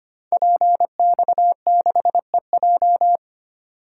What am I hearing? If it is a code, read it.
Morse code, heard as a keyed tone, about 25 wpm, 700 Hz. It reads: PX6EJ
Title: Code Groups - 5 Letter:Numbers - 25wpm